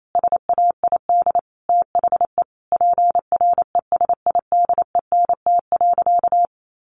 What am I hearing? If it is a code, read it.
Morse code, heard as a keyed tone, about 28 wpm, 700 Hz. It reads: SAID THE PRESIDENT.
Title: Sentences from Top 1500 Words 28wpm